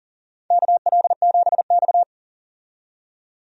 Morse code, heard as a keyed tone, about 40 wpm, 700 Hz. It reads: KL7X
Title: DXCC-100 Call Signs - 4 Characters - 40wpm